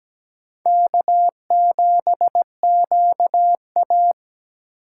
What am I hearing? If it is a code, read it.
Morse code, heard as a keyed tone, about 17 wpm, 700 Hz. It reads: K7QA
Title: USA Call Signs - 4 Characters - 4x Character Spacing - 17wpm